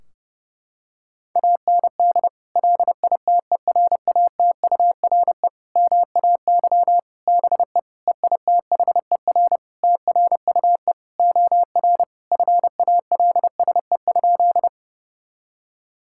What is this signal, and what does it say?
Morse code, heard as a keyed tone, about 30 wpm, 700 Hz. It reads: AND LITERATURE MAY BE EITHER TRUE OR FALSE?
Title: Sentences from Top 1600 Words 30wpm